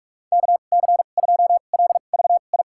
Morse code, heard as a keyed tone, about 46 wpm, spaced as for 32 wpm, 700 Hz. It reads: KC2LVI
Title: USA Call Signs - 6 Characters - 2x Character Spacing - 45wpm